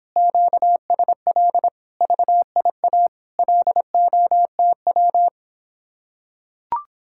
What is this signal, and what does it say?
Morse code, heard as a keyed tone, about 26 wpm, 700 Hz. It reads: QSL VIA LOTW